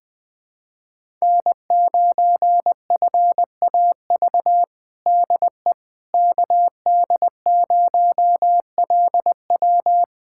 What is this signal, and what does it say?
Morse code, heard as a keyed tone, about 20 wpm, 700 Hz. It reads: N9FAV DE KD0LW
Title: USA Callsign DE Callsign - 5 Characters - 20wpm